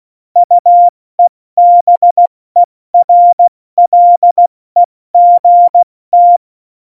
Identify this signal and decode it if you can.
Morse code, heard as a keyed tone, about 16 wpm, 700 Hz. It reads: UEBERLEGT
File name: Morsekurs / Witze / Witz 20. November 2018, 70 F80